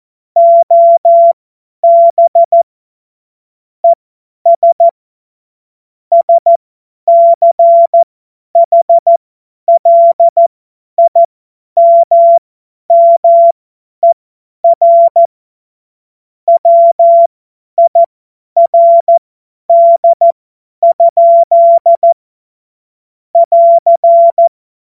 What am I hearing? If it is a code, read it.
Morse code, heard as a keyed tone, about 14 wpm, spaced as for 10 wpm, 680 Hz. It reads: OB ES SCHLIMMER WIRD? <AR>
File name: Morsekurs / Witze / Witz 27. Juli 2021, 50 F70